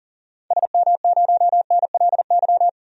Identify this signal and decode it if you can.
Morse code, heard as a keyed tone, about 40 wpm, 700 Hz. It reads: SM0NLY
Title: DXCC-100 Call Signs - 6 Characters - 40wpm